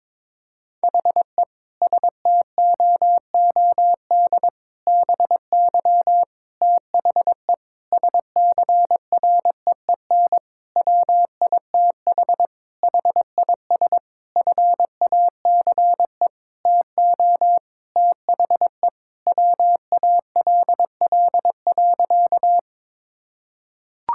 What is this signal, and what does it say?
Morse code, heard as a keyed tone, about 22 wpm, 700 Hz. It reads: HE STOOD BY THE SCREEN WITH HIS FACE TO THE WALL.
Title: Sentences from Top 1000 Words - Review Part 6 - 22wpm